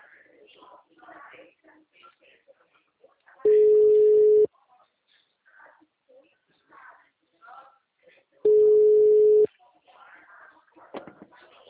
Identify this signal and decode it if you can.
Ringback tone (Europe, 425 Hz), with speech.
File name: out- -2010- - - .5362